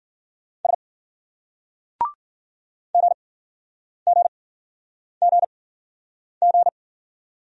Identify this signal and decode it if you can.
Morse code, heard as a keyed tone, about 48 wpm, spaced as for 9 wpm, 700 Hz. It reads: S GGGG
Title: Speed-Racing Single Letters 40wpm